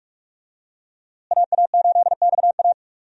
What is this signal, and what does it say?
Morse code, heard as a keyed tone, about 46 wpm, 700 Hz. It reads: AA8XA
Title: USA Call Signs - 5 Characters - Repeated 3x - 45wpm